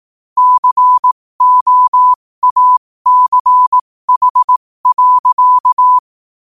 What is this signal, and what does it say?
Morse code, heard as a keyed tone, about 18 wpm, spaced as for 15 wpm, 1000 Hz. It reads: COACH.
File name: 15wpm